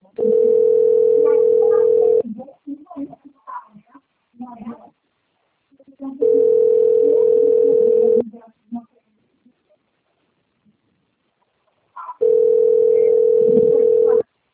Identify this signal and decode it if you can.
Ringback tone (North America), 440 + 480 Hz, with speech.
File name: rg-1600-unknown- - - .13526